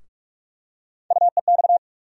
Morse code, heard as a keyed tone, about 46 wpm, 700 Hz. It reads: UEX
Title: USA Call Sign Postfixes - 3 Letters 45wpm